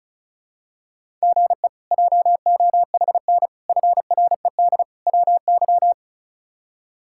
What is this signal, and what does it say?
Morse code, heard as a keyed tone, about 36 wpm, 700 Hz. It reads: GE JOHN FRED WY
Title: K1USN SST QSO Practice - 35wpm